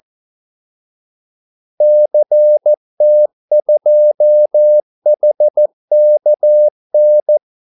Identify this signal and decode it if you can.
Morse code, heard as a keyed tone, about 14 wpm, 600 Hz. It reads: CT2HKN
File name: Callsigns-14wpm